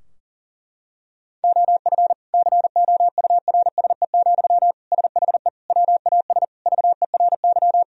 Morse code, heard as a keyed tone, about 40 wpm, 700 Hz. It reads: OF COURSE, SHE WAS VERY
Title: Sentences from Top 1000 Words - Review Part 4 - 40wpm